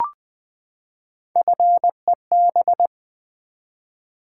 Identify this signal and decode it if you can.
Morse code, heard as a keyed tone, about 20 wpm, 700 Hz. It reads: FEB
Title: Build A Word - 3 Letters At A Time - 8-Letter Words - 20wpm